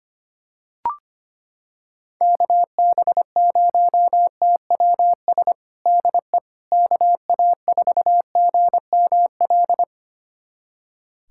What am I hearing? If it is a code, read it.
Morse code, heard as a keyed tone, about 25 wpm, 700 Hz. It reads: KB0TWS DE KA4GML
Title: USA Callsign DE Callsign - 6 Characters - 25wpm